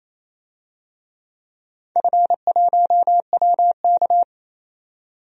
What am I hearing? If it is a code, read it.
Morse code, heard as a keyed tone, about 27 wpm, 700 Hz. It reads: F1WK
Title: DXCC-100 Call Signs - 4 Characters - 28wpm